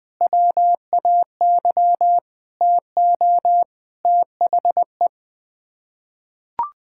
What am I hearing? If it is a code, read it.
Morse code, heard as a keyed tone, about 21 wpm, 700 Hz. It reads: WAY TO THE
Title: Sets of 9 Words - 20wpm